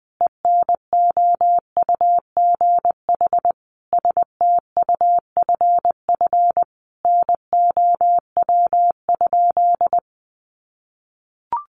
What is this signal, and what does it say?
Morse code, heard as a keyed tone, about 20 wpm, 700 Hz. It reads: ENOUGH STUFF NOW?
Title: Sentences from Top 1400 Words 20wpm